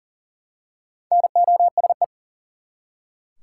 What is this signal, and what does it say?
Morse code, heard as a keyed tone, about 40 wpm, 700 Hz. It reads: NOSE
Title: Four Letter Words - 40wpm